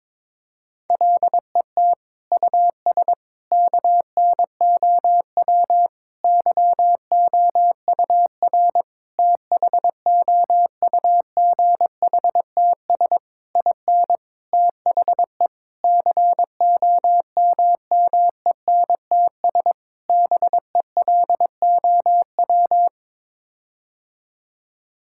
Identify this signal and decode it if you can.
Morse code, heard as a keyed tone, about 22 wpm, 700 Hz. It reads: LET US KNOW YOUR THOUGHTS IN THE COMMENTS BELOW
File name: Sets of 9 Words - 22wpm